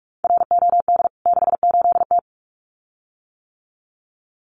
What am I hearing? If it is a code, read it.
Morse code, heard as a keyed tone, about 44 wpm, 700 Hz. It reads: ROD 68T
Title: CWT Exchange 45wpm